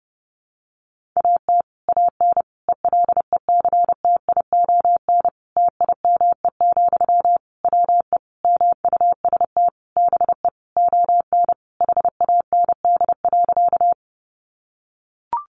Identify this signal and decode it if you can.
Morse code, heard as a keyed tone, about 30 wpm, 700 Hz. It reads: AT AN ELECTION TIME, WE MUST BE ON HAND.
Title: Sentences from Top 1500 Words 30wpm